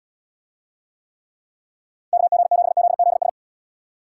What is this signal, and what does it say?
Morse code, heard as a keyed tone, about 78 wpm, 700 Hz. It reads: KG7ZZS